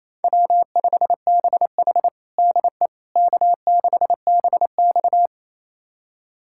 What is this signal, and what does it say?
Morse code, heard as a keyed tone, about 28 wpm, 700 Hz. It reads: W5BH DE K6BX
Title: USA Callsign DE Callsign - 4 Characters - 28wpm